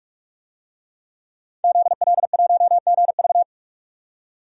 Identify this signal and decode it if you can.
Morse code, heard as a keyed tone, about 47 wpm, 700 Hz. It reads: ZL1GV
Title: DXCC-100 Call Signs - 5 Characters - 45wpm